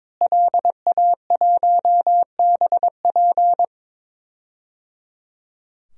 Morse code, heard as a keyed tone, about 21 wpm, 700 Hz. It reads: LA1BP
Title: DXCC-100 Call Signs - 5 Characters - Repeated 3x - 22wpm